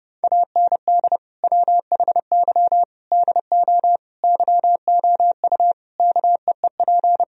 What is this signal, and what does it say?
Morse code, heard as a keyed tone, about 30 wpm, 700 Hz. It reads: AND WHY DO YOU KEEP
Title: Sentences from Top 1000 Words - Review Part 4 - 30wpm